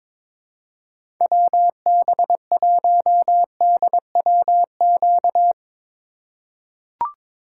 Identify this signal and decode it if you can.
Morse code, heard as a keyed tone, about 22 wpm, 700 Hz. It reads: WB1DWQ
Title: USA Call Signs - 6 Characters - 4x Character Spacing - 22wpm